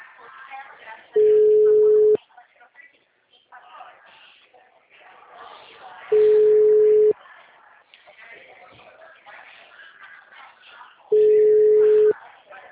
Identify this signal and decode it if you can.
Ringback tone (Europe, 425 Hz), with speech.
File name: out- -2017- - - .34798